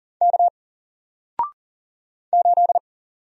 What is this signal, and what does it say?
Morse code, heard as a keyed tone, about 40 wpm, 700 Hz. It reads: K 8
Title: Single Letter-Number Alt 40wpm